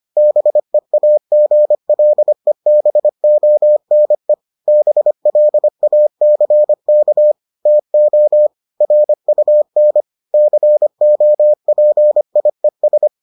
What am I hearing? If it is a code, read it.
Morse code, heard as a keyed tone, about 25 wpm, 600 Hz. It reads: BEAGLEBONE BLACK TO RUN COPIES